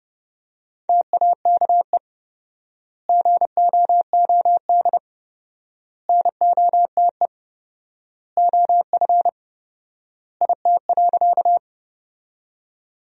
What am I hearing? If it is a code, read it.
Morse code, heard as a keyed tone, about 32 wpm, 700 Hz. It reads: TAKE GOOD NOTE OF IT.
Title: 4x Word-Spacing - Sentences from Top 500 Words - 30wpm